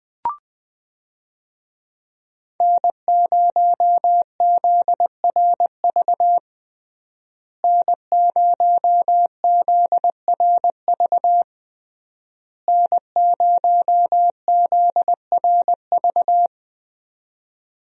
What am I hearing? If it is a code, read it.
Morse code, heard as a keyed tone, about 21 wpm, 700 Hz. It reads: N0ZRV N0ZRV N0ZRV
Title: USA Call Signs - 5 Characters - Repeated 3x - 20wpm